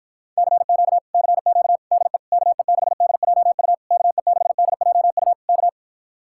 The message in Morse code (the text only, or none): CQ CQ DE KE6DJU KE6DJU K